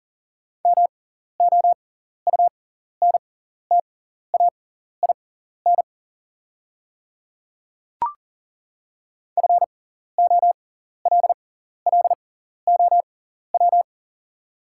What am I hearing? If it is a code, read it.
Morse code, heard as a keyed tone, about 37 wpm, spaced as for 14 wpm, 690 Hz. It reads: MOUNTAIN FOLLOW
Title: 6x Character Spacing - Top 500 Words - 40wpm